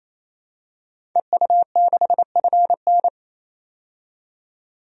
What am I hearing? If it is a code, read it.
Morse code, heard as a keyed tone, about 27 wpm, 700 Hz. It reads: EU6FN
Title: DXCC-100 Call Signs - 5 Characters - 28wpm